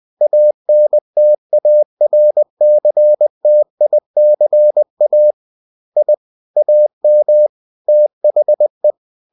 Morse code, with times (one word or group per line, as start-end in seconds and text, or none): 0.2-5.3 ANTARCTICA
6.0-6.1 I
6.6-7.5 AM
7.9-8.9 THE